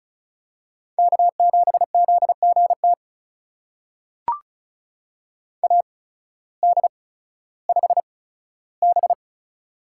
Morse code, heard as a keyed tone, about 36 wpm, spaced as for 16 wpm, 700 Hz. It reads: K7ZGT AD5B
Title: USA Call Signs - 8x Character Spacing - 35wpm